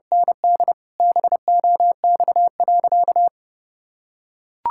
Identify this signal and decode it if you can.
Morse code, heard as a keyed tone, about 30 wpm, 700 Hz. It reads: ND BOX.
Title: Sentences from Top 1600 Words 30wpm